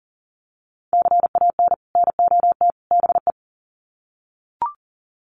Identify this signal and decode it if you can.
Morse code, heard as a keyed tone, about 40 wpm, 700 Hz. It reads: CAN NOT BE
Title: Sets of 3 Words 40wpm